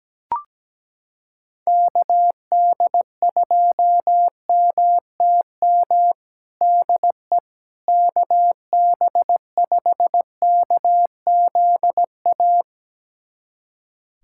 Morse code, heard as a keyed tone, about 17 wpm, 700 Hz. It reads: KD2MTM DE KB5KZA
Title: USA Callsign DE Callsign - 6 Characters - 17wpm